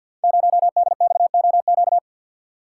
Morse code, heard as a keyed tone, about 46 wpm, 700 Hz. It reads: 0DXOQ
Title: Code Groups - 5 Letter:Numbers - 50wpm